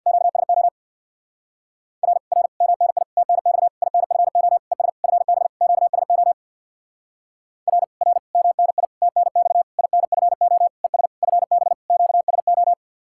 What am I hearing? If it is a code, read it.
Morse code, heard as a keyed tone, about 64 wpm, 700 Hz. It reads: QSO R R MNI TNX INFO ES FB QSO R R MNI TNX INFO ES FB QSO